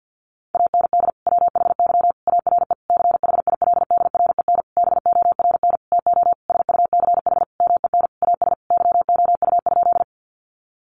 Morse code, heard as a keyed tone, about 50 wpm, 700 Hz. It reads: AND WHY ARE CHILDREN BORN TO SUCH MEN AS YOU?
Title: Sentences from Top 1000 Words 50wpm